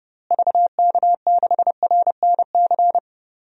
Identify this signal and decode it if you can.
Morse code, heard as a keyed tone, about 32 wpm, 700 Hz. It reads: VK6RNC
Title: DXCC-100 Call Signs - 6 Characters - Repeated 3x - 30wpm